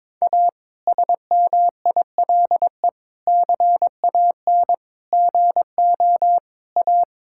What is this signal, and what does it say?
Morse code, heard as a keyed tone, about 21 wpm, 700 Hz. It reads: A SMILE CAN GO A
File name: Difficult Sentences with Hints - 22wpm